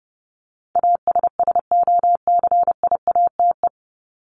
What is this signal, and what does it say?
Morse code, heard as a keyed tone, about 30 wpm, 700 Hz. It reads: ASSOCIATE
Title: ED Suffix - Top 100 Words with Base Words 30wpm